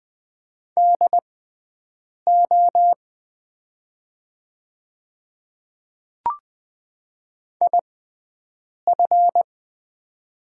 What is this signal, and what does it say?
Morse code, heard as a keyed tone, about 20 wpm, spaced as for 7 wpm, 700 Hz. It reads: DO IF
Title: 6x Character Spacing - Top 100 Words - 20wpm